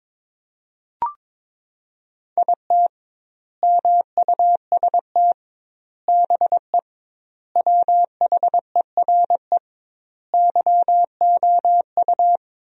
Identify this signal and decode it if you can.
Morse code, heard as a keyed tone, about 21 wpm, 700 Hz. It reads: IT MUST BE WHERE YOU
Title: 2x Word-Spacing - Sentences from Top 200 Words - 22wpm